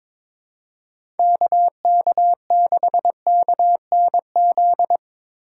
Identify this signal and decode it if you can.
Morse code, heard as a keyed tone, about 23 wpm, 700 Hz. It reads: K K 6 K N Z
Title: USA Call Signs - 6 Characters - 8x Character Spacing - 22wpm